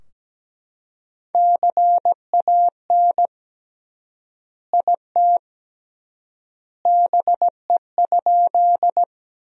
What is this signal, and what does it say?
Morse code, heard as a keyed tone, about 17 wpm, 700 Hz. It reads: CAN IT BE?
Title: 3x Word-Spacing - Sentences from Top 100 Words - 17wpm